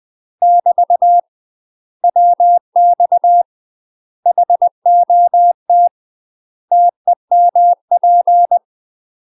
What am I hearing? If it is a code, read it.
Morse code, heard as a keyed tone, about 20 wpm, 700 Hz. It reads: <BT> WX HOT TEMP
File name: UK MASTER QSOs TRAINING / 20WPM / UK MASTER QSOs 0023